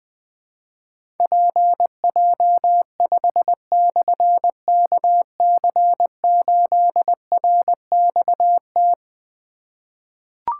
Morse code, heard as a keyed tone, about 20 wpm, 700 Hz. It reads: PJ5/KC8RXT
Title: Difficult Call Signs 20wpm